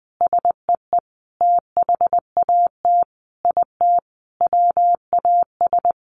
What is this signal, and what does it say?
Morse code, heard as a keyed tone, about 19 wpm, 700 Hz. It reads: SEE THAT IT WAS
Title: Sentences from Top 800 Words 20wpm